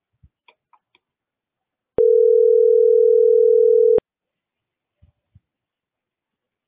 Ringback tone (North America), 440 + 480 Hz.